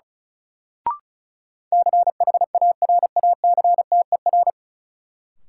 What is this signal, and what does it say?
Morse code, heard as a keyed tone, about 36 wpm, 700 Hz. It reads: CHARACTER
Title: Sentences from Top 1200 Words - New Word Drill Alt - 35wpm